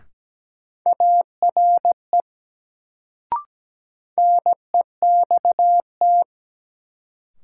Morse code, heard as a keyed tone, about 17 wpm, 700 Hz. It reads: ARE NEXT
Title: Sets of 2 Words - New Word Drill - 17wpm